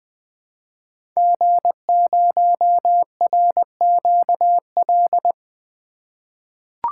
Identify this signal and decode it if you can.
Morse code, heard as a keyed tone, about 20 wpm, 700 Hz. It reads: G0RQL
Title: 2023 SOTA Chaser Call Signs - 20wpm